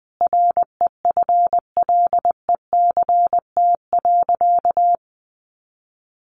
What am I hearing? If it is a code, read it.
Morse code, heard as a keyed tone, about 20 wpm, 700 Hz. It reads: REFLECT.
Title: Sentences from Top 1500 Words 20wpm